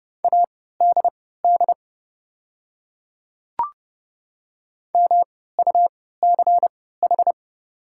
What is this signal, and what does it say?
Morse code, heard as a keyed tone, about 31 wpm, spaced as for 17 wpm, 700 Hz. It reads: ADD MUCH
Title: 3x Character Spacing - Top 300 Words - 30wpm